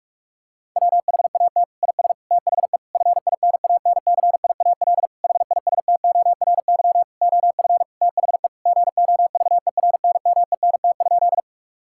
Morse code, heard as a keyed tone, about 46 wpm, 700 Hz. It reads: WHAT IS THE FINANCIAL HISTORY OF THE GOVERNMENT?
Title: Sentences from Top 1000 Words - Review Part 5 - 45wpm